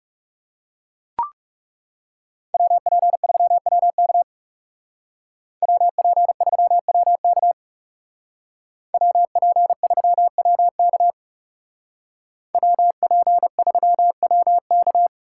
Morse code, heard as a keyed tone, about 37 wpm, 700 Hz. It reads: WP3WK WP3WK WP3WK WP3WK
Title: Speed Racing - USA 2x2 Call Signs 30wpm